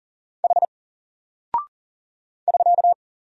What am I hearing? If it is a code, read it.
Morse code, heard as a keyed tone, about 40 wpm, 700 Hz. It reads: H <SK>
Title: Single Character (Letter:Number:Punctuation:Prosign) 40wpm